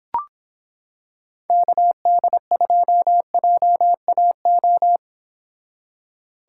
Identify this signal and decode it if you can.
Morse code, heard as a keyed tone, about 26 wpm, 700 Hz. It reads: KD2JAO